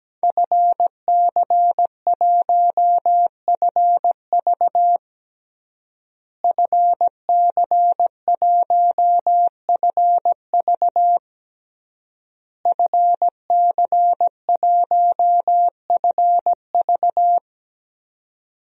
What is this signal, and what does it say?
Morse code, heard as a keyed tone, about 17 wpm, 700 Hz. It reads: FC1FV FC1FV FC1FV
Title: DXCC-100 Call Signs - 5 Characters - Repeated 3x - 17wpm